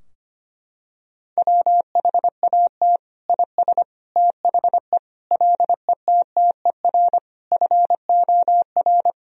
Morse code, heard as a keyed tone, about 25 wpm, 700 Hz. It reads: WHAT IS THE LETTER FOR
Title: Sentences from Top 1000 Words - Review Part 2 - 25wpm